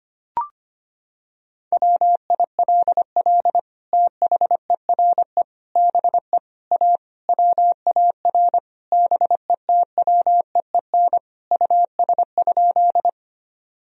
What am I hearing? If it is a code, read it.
Morse code, heard as a keyed tone, about 25 wpm, 700 Hz. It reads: WILL THERE BE A WAR BETWEEN US?
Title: Sentences from Top 1000 Words - Review Part 2 - 25wpm